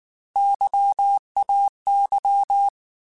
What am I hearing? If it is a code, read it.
Morse code, heard as a keyed tone, about 19 wpm, 800 Hz. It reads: YAY
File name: Alphabetical / Alphabetical mp3s / Ys @18